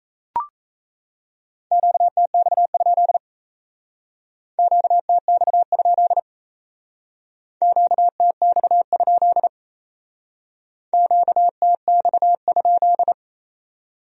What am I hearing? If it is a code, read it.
Morse code, heard as a keyed tone, about 34 wpm, 700 Hz. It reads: QTX? QTX? QTX? QTX?
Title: Speed-Racing Q-Codes - 28wpm